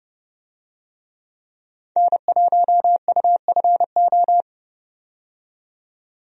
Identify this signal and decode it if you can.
Morse code, heard as a keyed tone, about 30 wpm, 700 Hz. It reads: N1UFO
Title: USA Call Signs - 5 Characters - 30wpm